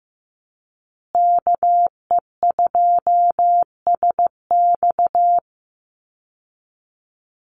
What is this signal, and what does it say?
Morse code, heard as a keyed tone, about 15 wpm, 700 Hz. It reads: KE2SX
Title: Call Signs Repeated 2x - 15wpm